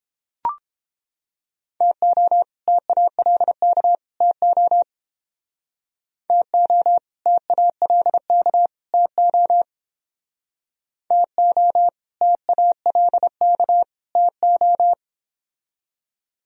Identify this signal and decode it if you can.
Morse code, heard as a keyed tone, about 31 wpm, 700 Hz. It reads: TO TALK TO TO TALK TO TO TALK TO